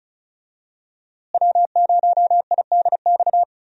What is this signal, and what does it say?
Morse code, heard as a keyed tone, about 36 wpm, 700 Hz. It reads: W0IDX
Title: USA Call Signs - 5 Characters - Repeated 3x - 35wpm